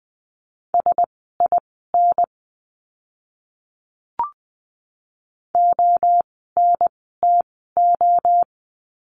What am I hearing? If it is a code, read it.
Morse code, heard as a keyed tone, about 20 wpm, spaced as for 15 wpm, 700 Hz. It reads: SIN ONTO
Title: 036 Lesson 7f - Letter s - Words 20wpm